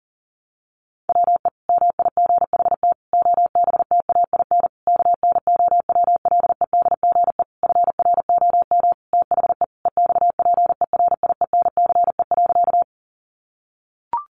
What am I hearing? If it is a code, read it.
Morse code, heard as a keyed tone, about 40 wpm, 700 Hz. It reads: WE MIGHT OBTAIN KNOWLEDGE FROM THE EXPERIENCE.
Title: Sentences from Top 1300 Words 40wpm